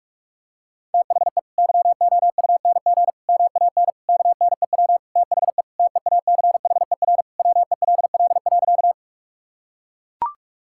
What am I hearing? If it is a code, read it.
Morse code, heard as a keyed tone, about 42 wpm, 700 Hz. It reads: THE YOUNG MAN KNEW THE TEACHER WELL.
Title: Sentences from Top 1000 Words - Review Part 6 - 45wpm